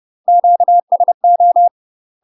Morse code, heard as a keyed tone, about 30 wpm, 700 Hz. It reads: QSO
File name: UK MASTER QSOs TRAINING / 30WPM / UK MASTER QSOs 0002